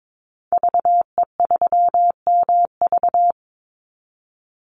Morse code, heard as a keyed tone, about 22 wpm, 700 Hz. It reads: VE3MV
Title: Call Signs Repeated 3x - 22wpm